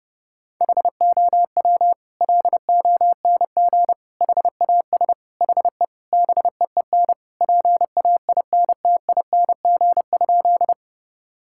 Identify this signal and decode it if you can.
Morse code, heard as a keyed tone, about 32 wpm, 700 Hz. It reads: HOW LONG HAS HE BEEN PAINTING?
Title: Sentences from Top 1000 Words - Review Part 6 - 30wpm